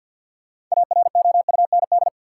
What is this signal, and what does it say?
Morse code, heard as a keyed tone, about 49 wpm, 700 Hz. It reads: AROUND
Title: Sets of 2 Words - New Word Drill - 50wpm